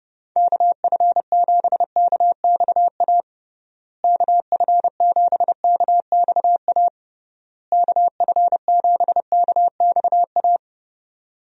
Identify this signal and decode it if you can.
Morse code, heard as a keyed tone, about 30 wpm, 700 Hz. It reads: KF7KXA KF7KXA KF7KXA
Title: USA Call Signs - 6 Characters - Repeated 3x - 30wpm